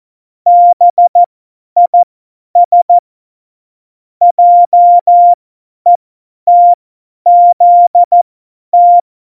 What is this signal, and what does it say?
Morse code, heard as a keyed tone, about 14 wpm, spaced as for 10 wpm, 700 Hz. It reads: BIS JETZT
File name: Morsekurs / Witze / Witz 17.Dezember 2019, 50 F70